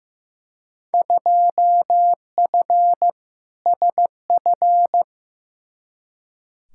Morse code, heard as a keyed tone, about 15 wpm, 700 Hz. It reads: 2F SF
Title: Field Day Exchanges - 15wpm